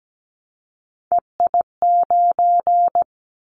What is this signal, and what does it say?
Morse code, heard as a keyed tone, about 17 wpm, 700 Hz. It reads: EI9
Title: Callsign Prefixes for Top 50 DXCC - 17wpm